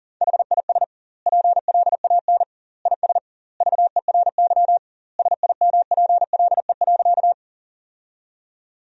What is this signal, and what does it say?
Morse code, heard as a keyed tone, about 40 wpm, 700 Hz. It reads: HIS PLAN IS VERY SIMPLE.
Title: 2x Word-Spacing - Sentences from Top 500 Words - 40wpm